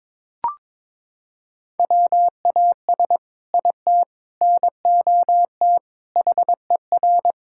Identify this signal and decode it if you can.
Morse code, heard as a keyed tone, about 22 wpm, 700 Hz. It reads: WAS IT NOT HER
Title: Sentences from Top 1600 Words 22wpm